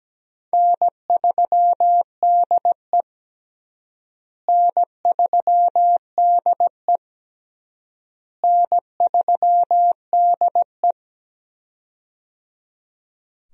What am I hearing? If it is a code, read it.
Morse code, heard as a keyed tone, about 17 wpm, 700 Hz. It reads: N3DE N3DE N3DE
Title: USA Call Signs - 4 Characters - Repeated 3x - 17wpm